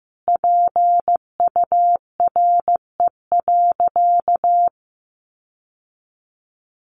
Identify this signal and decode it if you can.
Morse code, heard as a keyed tone, about 15 wpm, 700 Hz. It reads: PURE.
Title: Sentences from Top 1500 Words 15wpm